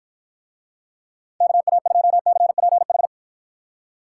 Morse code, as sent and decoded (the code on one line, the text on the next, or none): -.- .- ..--- -.-. .--. ....
KA2CPH